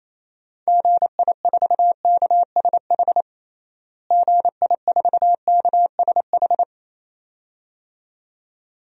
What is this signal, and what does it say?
Morse code, heard as a keyed tone, about 28 wpm, 700 Hz. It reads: GI4KSH GI4KSH
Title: DXCC-100 Call Signs - 6 Characters - Repeated 2x - 28wpm